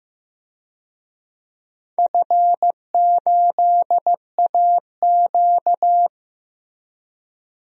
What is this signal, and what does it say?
Morse code, heard as a keyed tone, about 15 wpm, 700 Hz. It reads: F8AQ
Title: DXCC-100 Call Signs - 4 Characters - Repeated 2x - 15wpm